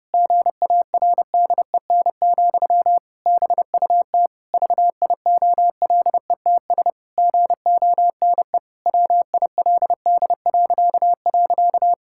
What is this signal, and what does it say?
Morse code, heard as a keyed tone, about 30 wpm, 700 Hz. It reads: GARDEN, BUT VIOLETS GONE WILD..
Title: Pangram Challenge v2 - 30wpm